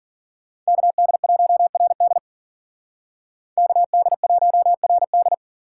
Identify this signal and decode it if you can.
Morse code, heard as a keyed tone, about 52 wpm, 700 Hz. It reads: KD1RD KD1RD